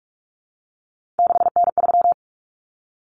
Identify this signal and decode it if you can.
Morse code, heard as a keyed tone, about 44 wpm, 700 Hz. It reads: 6N3
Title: Common Cut Numbers - 3-Digit 45wpm